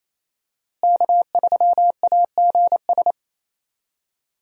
Morse code, heard as a keyed tone, about 27 wpm, 700 Hz. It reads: K3AGS
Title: USA Call Signs - 5 Characters - Repeated 3x - 28wpm